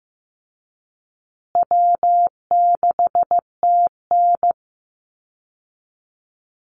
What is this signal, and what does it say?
Morse code, heard as a keyed tone, about 15 wpm, 700 Hz. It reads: W6TN
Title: Easy Call Signs Repeated 3x - 15wpm